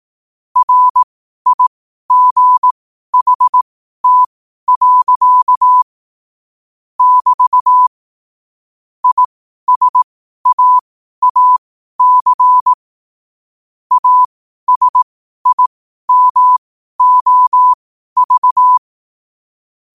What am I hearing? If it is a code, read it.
Morse code, heard as a keyed tone, about 18 wpm, spaced as for 12 wpm, 1000 Hz. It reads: RIGHT. <BT> ISAAC ASIMOV
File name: qotdfiles / 12wpm